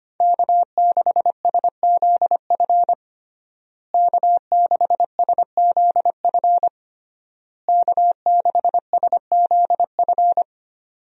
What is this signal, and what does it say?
Morse code, heard as a keyed tone, about 25 wpm, 700 Hz. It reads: K6SZF K6SZF K6SZF
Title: USA Call Signs - 5 Characters - Repeated 3x - 25wpm